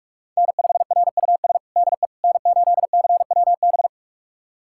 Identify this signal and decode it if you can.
Morse code, heard as a keyed tone, about 46 wpm, 700 Hz. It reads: N5RUS DE N8CWB
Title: USA Callsign DE Callsign - 5 Characters - 45wpm